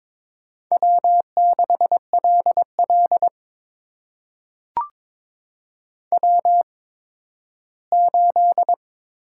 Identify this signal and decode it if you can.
Morse code, heard as a keyed tone, about 23 wpm, spaced as for 13 wpm, 700 Hz. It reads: W6LL W8
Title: USA Call Signs - 4 Characters - 8x Character Spacing - 22wpm